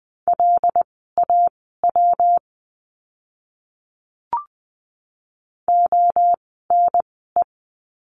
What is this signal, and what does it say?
Morse code, heard as a keyed tone, about 20 wpm, spaced as for 15 wpm, 700 Hz. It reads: LAW ONE
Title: 115 Lesson 19f - Letter w - Words 20wpm